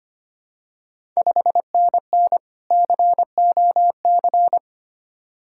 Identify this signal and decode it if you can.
Morse code, heard as a keyed tone, about 25 wpm, 700 Hz. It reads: 5NN COC
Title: Ontario QSO Party Exchange 25wpm